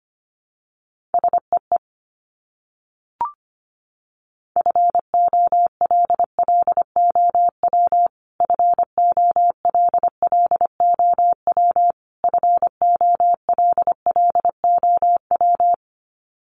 Morse code, heard as a keyed tone, about 25 wpm, 700 Hz. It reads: SEE FOLLOW FOLLOW FOLLOW
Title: Top 500 Words Repeated 25wpm